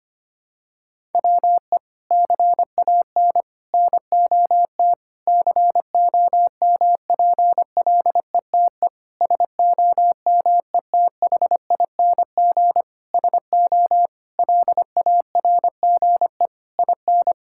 Morse code, heard as a keyed tone, about 25 wpm, 700 Hz. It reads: WE CAN NOT COMPLETE SOMETHING SO LARGE IN
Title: Sentences from Top 1000 Words - Review Part 6 - 25wpm